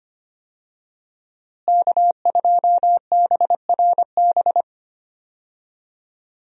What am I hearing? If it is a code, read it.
Morse code, heard as a keyed tone, about 25 wpm, 700 Hz. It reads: K2BRB
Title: USA Call Signs - 5 Characters - Repeated 2x - 25wpm